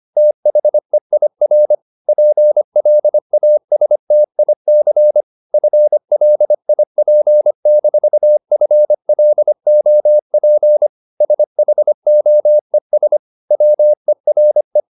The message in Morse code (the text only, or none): THEIR PLASTIC FLIP-FLOP SHOES WERE